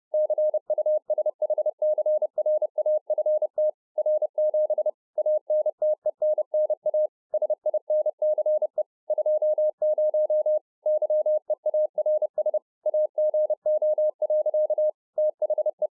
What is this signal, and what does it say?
Morse code, heard as a keyed tone, about 30 wpm, 610 Hz. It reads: CUSHCRAFT R7 ANTENNA SINCE 20 YEARS AGO. THE